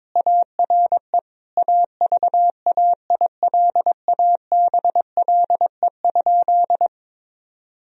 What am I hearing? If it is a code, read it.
Morse code, heard as a keyed tone, about 21 wpm, 700 Hz. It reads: ARE AVAILABLE?
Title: Sentences from Top 1000 Words - Review Part 3 - 22wpm